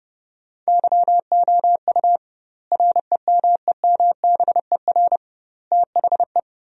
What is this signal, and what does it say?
Morse code, heard as a keyed tone, about 30 wpm, 700 Hz. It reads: YOU REMEMBER THE
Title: 2x Word-Spacing - Sentences from Top 400 Words - 30wpm